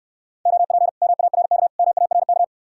Morse code, heard as a keyed tone, about 69 wpm, 700 Hz. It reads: OP MARK MARK